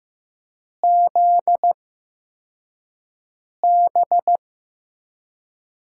Morse code, heard as a keyed tone, about 15 wpm, spaced as for 4 wpm, 700 Hz. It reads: ZB
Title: USA Call Signs - 8x Character Spacing - 15wpm